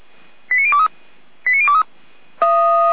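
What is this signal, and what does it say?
Five-tone selective call (ZVEI), heard as quick runs of five tones, tones of about 70 ms. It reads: 89023 89023
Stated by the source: ZVEI five-tone call received on SDR, recorded by feuerwehr-orlamuende.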